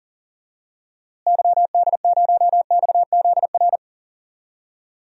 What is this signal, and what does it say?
Morse code, heard as a keyed tone, about 40 wpm, 700 Hz. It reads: YD0XZR
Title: DXCC-100 Call Signs - 6 Characters - Repeated 2x - 40wpm